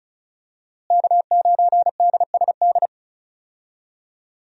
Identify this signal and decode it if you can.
Morse code, heard as a keyed tone, about 36 wpm, 700 Hz. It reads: K9DSD
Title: USA Call Signs - 5 Characters - Repeated 3x - 35wpm